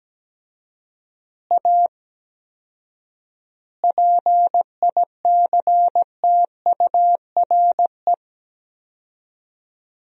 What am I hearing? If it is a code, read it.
Morse code, heard as a keyed tone, about 17 wpm, 700 Hz. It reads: A PICTURE
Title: 4x Word-Spacing - Sentences from Top 500 Words - 17wpm